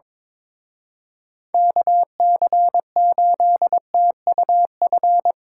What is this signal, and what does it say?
Morse code, heard as a keyed tone, about 25 wpm, 700 Hz. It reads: KC8TUF
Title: Speed Racing - USA 2x3 Call Signs 22wpm